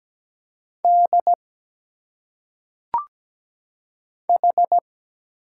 Morse code, heard as a keyed tone, about 17 wpm, 700 Hz. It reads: D H
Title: Single Letters - Phonetic Pronunciation 17wpm